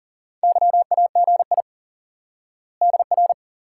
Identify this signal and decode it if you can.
Morse code, heard as a keyed tone, about 40 wpm, 700 Hz. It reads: YAGI DR
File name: IQE Course 3 - Lesson 89 - YAGI - Two Elements - 40wpm